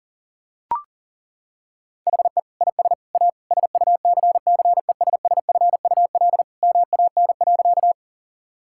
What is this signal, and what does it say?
Morse code, heard as a keyed tone, about 37 wpm, 700 Hz. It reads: HE IS A SUCCESSFUL MAN.
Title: Sentences from Top 1000 Words - Review Part 6 - 40wpm